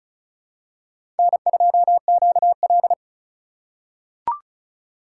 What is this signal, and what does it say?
Morse code, heard as a keyed tone, about 36 wpm, 700 Hz. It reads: N2QL
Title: USA Call Signs - 4 Characters - 2x Character Spacing - 35wpm